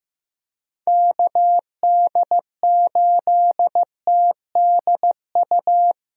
Morse code, heard as a keyed tone, about 15 wpm, 700 Hz. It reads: KD8TDU
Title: USA Call Signs - 6 Characters - Repeated 3x - 15wpm